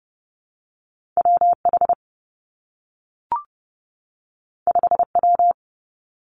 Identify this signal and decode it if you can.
Morse code, heard as a keyed tone, about 30 wpm, 700 Hz. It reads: W H 5 W
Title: Two Challenging Letter-Number Combinations 30wpm